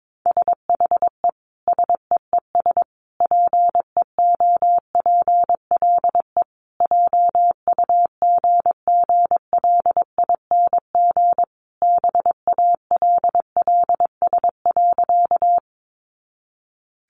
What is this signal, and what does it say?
Morse code, heard as a keyed tone, about 22 wpm, 700 Hz. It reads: SHE SEES PEOPLE JUGGLING BALLS.
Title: Easy Phrases 22wpm